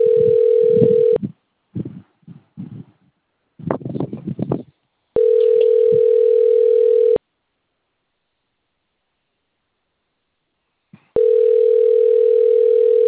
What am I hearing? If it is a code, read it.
Ringback tone (North America), 440 + 480 Hz.
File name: rg-1600-unknown- - - .27341